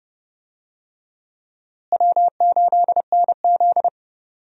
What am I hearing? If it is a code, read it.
Morse code, heard as a keyed tone, about 32 wpm, 700 Hz. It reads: W8NZ
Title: USA Call Signs - 4 Characters - Repeated 3x - 30wpm